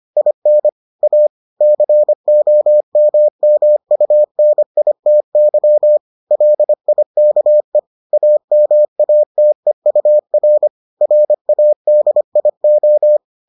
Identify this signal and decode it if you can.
Morse code, heard as a keyed tone, about 25 wpm, 600 Hz. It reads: IN A COMMUNITY LIKE AMATEUR RADIO